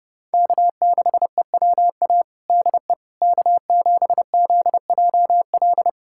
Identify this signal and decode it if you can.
Morse code, heard as a keyed tone, about 30 wpm, 700 Hz. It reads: K6EWA DE K7ZJL
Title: USA Callsign DE Callsign - 5 Characters - 30wpm